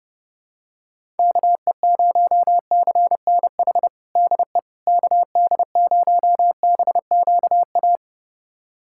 Morse code, heard as a keyed tone, about 32 wpm, 700 Hz. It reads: KE0CNH DE KD0BQA
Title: USA Callsign DE Callsign - 6 Characters - 30wpm